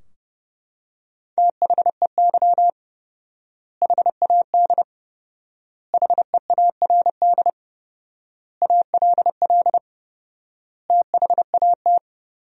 Morse code, heard as a keyed tone, about 30 wpm, 700 Hz. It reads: THEY HAD HEARD ALL THAT
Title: 4x Word-Spacing - Sentences from Top 400 Words - 30wpm